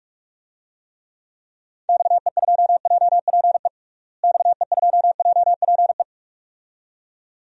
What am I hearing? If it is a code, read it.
Morse code, heard as a keyed tone, about 41 wpm, 700 Hz. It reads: XE2JPE XE2JPE
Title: DXCC-100 Call Signs - 6 Characters - Repeated 2x - 45wpm